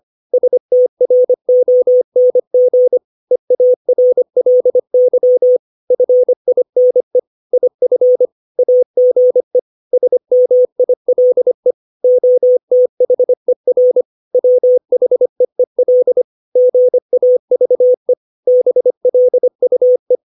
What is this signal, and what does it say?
Morse code, heard as a keyed tone, about 25 wpm, 500 Hz. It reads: STRONG EARLY FINE IF AGE SMILE OTHER WHEEL GAVE BLUE
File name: 632 words / 632 Words 25WPM